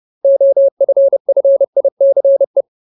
Morse code, heard as a keyed tone, about 30 wpm, 550 Hz. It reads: OFFICE
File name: Ultimate Word List / 30WPM Ultimate